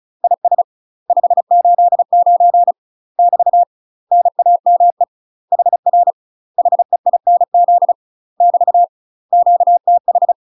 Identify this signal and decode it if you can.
Morse code, heard as a keyed tone, about 36 wpm, 700 Hz. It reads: IS 589 <BT> NAME HR HEINZ <BT> QTH
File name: UK MASTER QSOs TRAINING / 35WPM / UK MASTER QSOs 0035